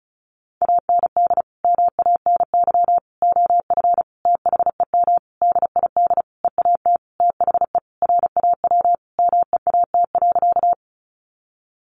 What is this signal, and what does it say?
Morse code, heard as a keyed tone, about 34 wpm, 700 Hz. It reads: AND MANY OF THEM DID EAT THE RAW MEAT.
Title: Sentences from Top 1500 Words 35wpm